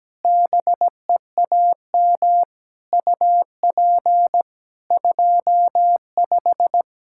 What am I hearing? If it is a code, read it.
Morse code, heard as a keyed tone, about 17 wpm, 700 Hz. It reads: BEAM UP 25
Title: K1USN SST QSO Practice 17wpm